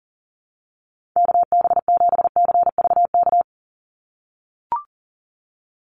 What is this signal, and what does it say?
Morse code, heard as a keyed tone, about 40 wpm, 700 Hz. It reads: KB7CVK
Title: USA Call Signs 40wpm